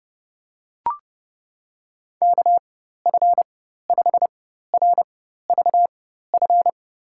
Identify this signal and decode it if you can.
Morse code, heard as a keyed tone, about 32 wpm, spaced as for 14 wpm, 700 Hz. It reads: KF5RVF
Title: USA Call Signs - 6 Characters - 4x Character Spacing - 30wpm